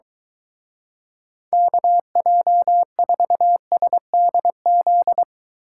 Morse code, heard as a keyed tone, about 23 wpm, 700 Hz. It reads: KJ4SDZ